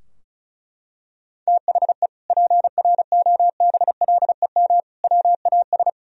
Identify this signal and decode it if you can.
Morse code, heard as a keyed tone, about 36 wpm, 700 Hz. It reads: THE PROBLEM WAS
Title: Sentences from Top 1000 Words - Review Part 2 - 35wpm